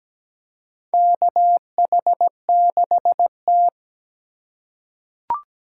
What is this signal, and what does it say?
Morse code, heard as a keyed tone, about 17 wpm, 700 Hz. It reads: KH6T
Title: USA Call Signs - 4 Characters - 2x Character Spacing - 17wpm